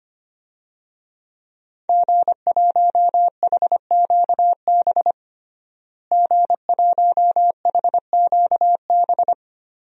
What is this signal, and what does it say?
Morse code, heard as a keyed tone, about 25 wpm, 700 Hz. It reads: G1HQB G1HQB
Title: DXCC-100 Call Signs - 5 Characters - Repeated 3x - 25wpm